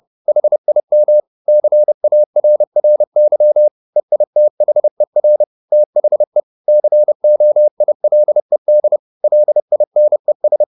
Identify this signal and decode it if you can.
Morse code, heard as a keyed tone, about 30 wpm, 600 Hz. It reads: HIM CARRY EITHER THE COILED LINES
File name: the old man and the sea / the old man and the sea 1 30wpm